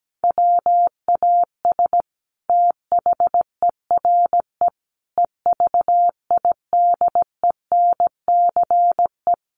Morse code, heard as a keyed tone, about 17 wpm, 700 Hz. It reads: WAS THERE EVIDENCE
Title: Sentences from Top 1300 Words 17wpm